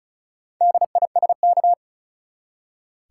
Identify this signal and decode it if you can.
Morse code, heard as a keyed tone, about 34 wpm, 700 Hz. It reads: DISK
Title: Four Letter Words - 35wpm